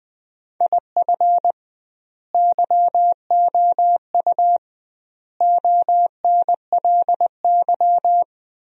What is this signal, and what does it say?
Morse code, heard as a keyed tone, about 20 wpm, 700 Hz. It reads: IF YOU ONLY
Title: 2x Word-Spacing - Sentences from Top 400 Words - 20wpm